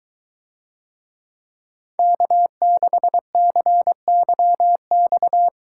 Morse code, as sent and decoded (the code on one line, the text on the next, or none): -.- -.... -.-. -.-- -..-
K6CYX